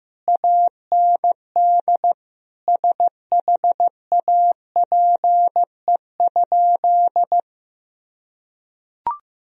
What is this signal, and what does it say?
Morse code, heard as a keyed tone, about 15 wpm, 700 Hz. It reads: AND SHAPE?
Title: Sentences from Top 1000 Words - Review Part 3 - 15wpm